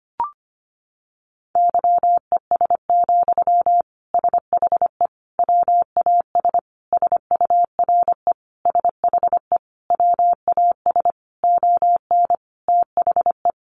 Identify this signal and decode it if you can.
Morse code, heard as a keyed tone, about 25 wpm, 690 Hz. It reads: YES, SHE WAS SURE SHE WAS ON THE
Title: Sentences from Top 1100 Words 25wpm